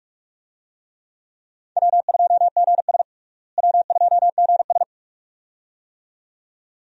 Morse code, heard as a keyed tone, about 44 wpm, 700 Hz. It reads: W2GS W2GS
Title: USA Call Signs - 4 Characters - Repeated 2x - 45wpm